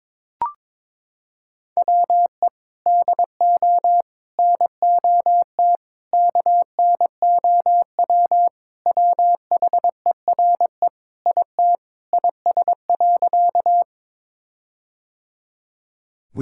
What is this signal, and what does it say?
Morse code, heard as a keyed tone, about 22 wpm, 700 Hz. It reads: WE DO NOT KNOW WHERE IT IS.
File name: practice / 22wpm / Sentences with 4-Word Phrases - 22wpm